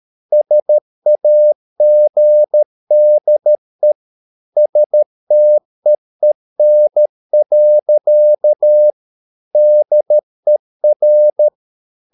Morse code, heard as a keyed tone, about 13 wpm, 600 Hz. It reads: SAGDE STEEN. DER
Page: Ferie 13 wpm